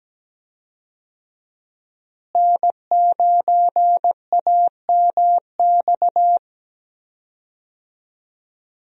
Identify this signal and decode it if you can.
Morse code, heard as a keyed tone, about 17 wpm, 700 Hz. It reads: N9AMX
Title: USA Call Signs - 5 Characters - Repeated 3x - 17wpm